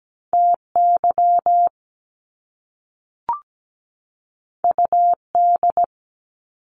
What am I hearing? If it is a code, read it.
Morse code, heard as a keyed tone, about 17 wpm, 700 Hz. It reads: TY UD
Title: Top Two Letter Combinations in Words 17wpm